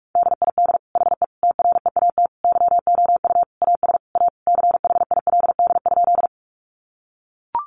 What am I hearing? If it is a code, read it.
Morse code, heard as a keyed tone, about 44 wpm, 700 Hz. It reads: DID HE TREAT YOU AS A CHILD?
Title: Sentences from Top 1300 Words 45wpm